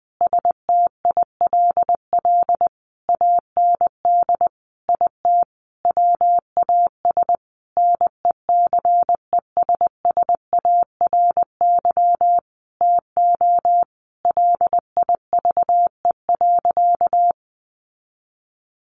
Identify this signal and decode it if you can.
Morse code, heard as a keyed tone, about 20 wpm, 700 Hz. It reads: STILL AND IT WAS NECESSARY TO LIVE.
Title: Sentences from Top 800 Words 20wpm